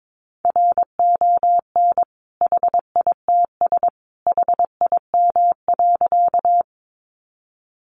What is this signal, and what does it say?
Morse code, heard as a keyed tone, about 22 wpm, 700 Hz. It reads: RON HITS HIM.
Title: Additional Easy Phrases 22wpm